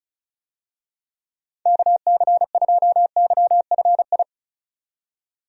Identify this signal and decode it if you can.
Morse code, heard as a keyed tone, about 36 wpm, 700 Hz. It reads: KC2YFI
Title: USA Call Signs - 6 Characters - 35wpm